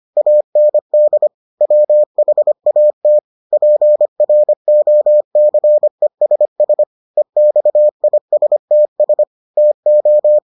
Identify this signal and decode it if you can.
Morse code, heard as a keyed tone, about 25 wpm, 600 Hz. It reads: AND WHAT PROCESS EXISTS TO